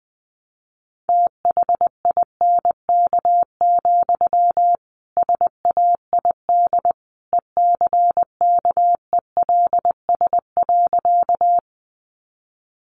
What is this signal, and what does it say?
Morse code, heard as a keyed tone, about 20 wpm, 700 Hz. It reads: THINK, SAID ECKELS.
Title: A Sound Of Thunder 20wpm